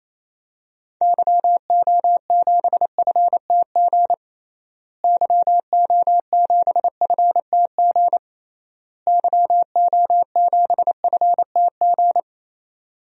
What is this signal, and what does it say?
Morse code, heard as a keyed tone, about 27 wpm, 700 Hz. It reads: YO7FTG YO7FTG YO7FTG
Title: DXCC-100 Call Signs - 6 Characters - Repeated 3x - 28wpm